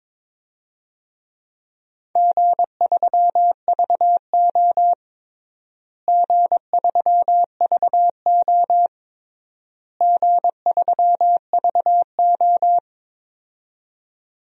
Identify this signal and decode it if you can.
Morse code, heard as a keyed tone, about 22 wpm, 700 Hz. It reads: G3VO G3VO G3VO
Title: DXCC-100 Call Signs - 4 Characters - Repeated 3x - 22wpm